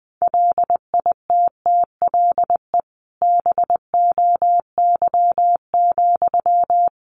Morse code, heard as a keyed tone, about 20 wpm, 700 Hz. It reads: LITTLE BOY,
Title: Sentences from Top 1000 Words 20wpm